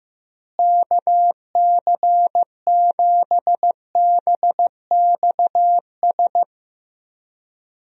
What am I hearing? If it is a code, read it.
Morse code, heard as a keyed tone, about 15 wpm, 700 Hz. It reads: KC7BXS
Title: USA Call Signs - 6 Characters - Repeated 3x - 15wpm